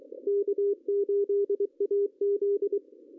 Morse code, heard as a keyed tone, about 22 wpm, 410 Hz. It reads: K8AZ
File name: CW / cqww-cw-k5tr